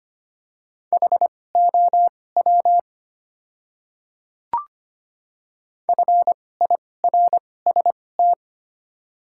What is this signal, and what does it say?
Morse code, heard as a keyed tone, about 25 wpm, spaced as for 18 wpm, 690 Hz. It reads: HOW FIRST
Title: 2x Character Spacing - Top 100 Words - 25wpm